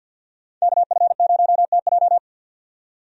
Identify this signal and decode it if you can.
Morse code, heard as a keyed tone, about 46 wpm, 700 Hz. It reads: KF0TJ
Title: USA Call Signs - 5 Characters - 50wpm